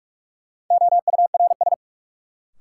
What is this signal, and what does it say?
Morse code, heard as a keyed tone, about 46 wpm, 700 Hz. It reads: OURS
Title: Sentences from Top 1000 Words - Review Part 5 - New Word Drill - 45wpm